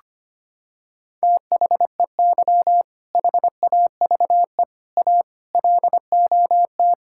Morse code, heard as a keyed tone, about 25 wpm, 700 Hz. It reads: THEY HAVE A LOT
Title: Sentences from Top 1000 Words - Review Part 5 - 25wpm